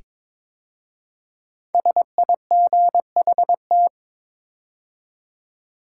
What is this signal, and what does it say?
Morse code, heard as a keyed tone, about 22 wpm, 700 Hz. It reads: SIGHT
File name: 5 Letter Words - First 3 Letters Given - 22wpm